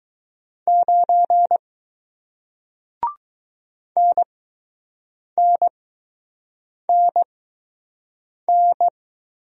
Morse code, heard as a keyed tone, about 20 wpm, spaced as for 6 wpm, 700 Hz. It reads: 9 NNNN